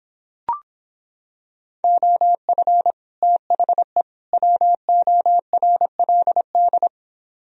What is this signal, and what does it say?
Morse code, heard as a keyed tone, about 26 wpm, 700 Hz. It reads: OF THE WORLD